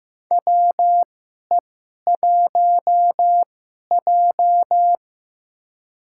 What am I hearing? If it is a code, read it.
Morse code, heard as a keyed tone, about 15 wpm, spaced as for 11 wpm, 700 Hz. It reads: WE1J
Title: USA Call Signs - 2x Character Spacing - 15wpm